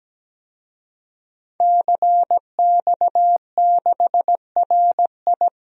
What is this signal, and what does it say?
Morse code, heard as a keyed tone, about 17 wpm, 700 Hz. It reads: CX6RI
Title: DXCC-100 Call Signs - 5 Characters - 17wpm